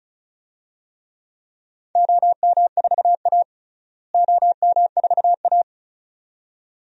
Morse code, heard as a keyed tone, about 34 wpm, 700 Hz. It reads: OM4A OM4A
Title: DXCC-100 Call Signs - 4 Characters - Repeated 2x - 35wpm